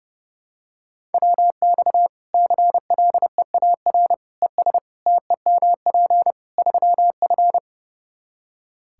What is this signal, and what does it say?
Morse code, heard as a keyed tone, about 30 wpm, 700 Hz. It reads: WX CLEAR ES TEMP 3F
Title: Practice QSO - Exchange 2 - Info - 30wpm